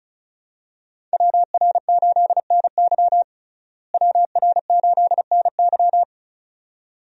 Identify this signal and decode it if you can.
Morse code, heard as a keyed tone, about 36 wpm, 700 Hz. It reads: WR8NY WR8NY
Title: USA Call Signs - 5 Characters - Repeated 2x - 35wpm